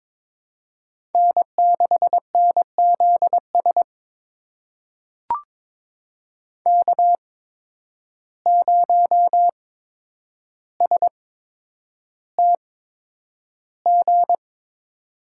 Morse code, heard as a keyed tone, about 22 wpm, spaced as for 9 wpm, 700 Hz. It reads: N6NZS K0STG
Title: USA Call Signs - 5 Characters - 8x Character Spacing - 22wpm